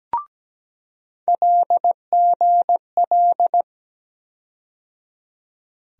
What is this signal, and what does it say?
Morse code, heard as a keyed tone, about 17 wpm, 700 Hz. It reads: LGL
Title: USA Call Sign Postfixes - 3 Letters 17wpm